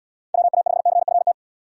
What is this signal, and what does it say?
Morse code, heard as a keyed tone, about 72 wpm, 690 Hz. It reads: WA5CFT